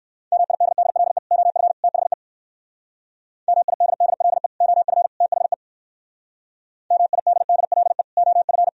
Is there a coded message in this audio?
Yes: Morse code from 0.3 to 8.7 s, heard as a keyed tone, about 65 wpm, 700 Hz. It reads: MIDDLE OF THE MIDDLE OF THE MIDDLE OF